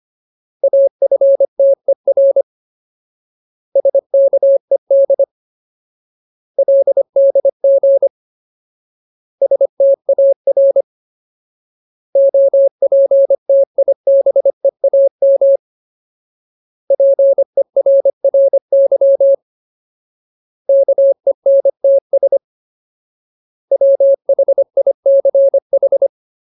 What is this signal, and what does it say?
Morse code, heard as a keyed tone, about 25 wpm, 550 Hz. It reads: AFTER SKED LDG STAR OPTIBEAM PERRY KENTS WHICH
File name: Ultimate Word List / 25WPM Ultimate